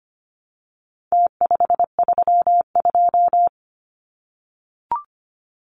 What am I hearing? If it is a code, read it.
Morse code, heard as a keyed tone, about 25 wpm, 700 Hz. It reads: T532
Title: Numbers - Contest Style - 4-Digit 25wpm